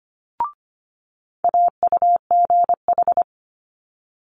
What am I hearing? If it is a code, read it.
Morse code, heard as a keyed tone, about 25 wpm, 700 Hz. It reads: AUGH
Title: Multi-letter Phonemes Quick Fire 25wpm